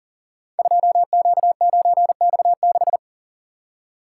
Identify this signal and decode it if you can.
Morse code, heard as a keyed tone, about 41 wpm, 700 Hz. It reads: 2Q9X6
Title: Code Groups - 5 Letter:Numbers - 40wpm